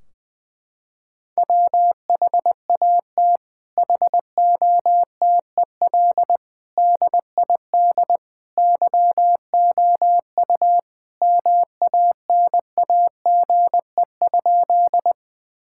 Morse code, heard as a keyed tone, about 20 wpm, 700 Hz. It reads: WHAT HOTEL DID YOU MANAGE?
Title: Sentences from Top 1600 Words 20wpm